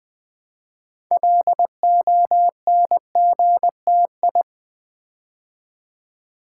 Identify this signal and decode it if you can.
Morse code, heard as a keyed tone, about 20 wpm, 700 Hz. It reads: LONGTI
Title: Build A Word - 3 Letters At A Time - 8-Letter Words - 20wpm